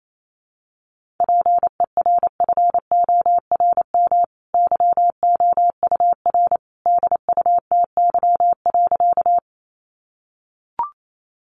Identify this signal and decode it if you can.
Morse code, heard as a keyed tone, about 28 wpm, 700 Hz. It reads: PERFORM YOUR DUTY.
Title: Sentences from Top 1500 Words 28wpm